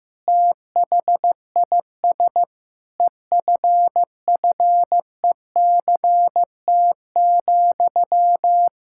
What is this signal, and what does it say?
Morse code, heard as a keyed tone, about 15 wpm, 700 Hz. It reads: THIS EFFECT,
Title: Sentences from Top 1600 Words 15wpm